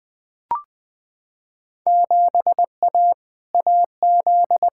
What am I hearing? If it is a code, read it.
Morse code, heard as a keyed tone, about 20 wpm, 700 Hz. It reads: 7A AZ
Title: Field Day Exchanges - 20wpm